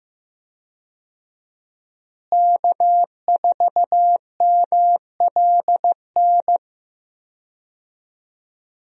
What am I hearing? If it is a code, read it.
Morse code, heard as a keyed tone, about 15 wpm, 700 Hz. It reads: K4MLN
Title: USA Call Signs - 5 Characters - Repeated 3x - 15wpm